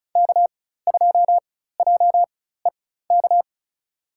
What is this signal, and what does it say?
Morse code, heard as a keyed tone, about 36 wpm, spaced as for 16 wpm, 700 Hz. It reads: K2JEK
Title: USA Call Signs - 4x Character Spacing - 35wpm